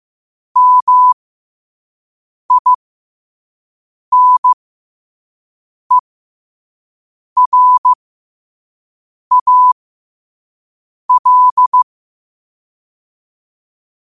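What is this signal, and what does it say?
Morse code, heard as a keyed tone, about 15 wpm, spaced as for 5 wpm, 1000 Hz. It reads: MINERAL
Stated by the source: Morse code quote of the day at 5 wpm.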